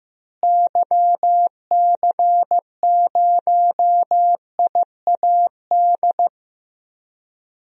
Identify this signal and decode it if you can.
Morse code, heard as a keyed tone, about 15 wpm, 700 Hz. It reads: YC0IAD
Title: DXCC-100 Call Signs - 6 Characters - Repeated 2x - 15wpm